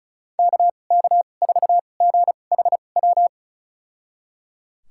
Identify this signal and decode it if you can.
Morse code, heard as a keyed tone, about 36 wpm, spaced as for 25 wpm, 700 Hz. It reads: KK4GHW
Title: USA Call Signs - 6 Characters - 2x Character Spacing - 35wpm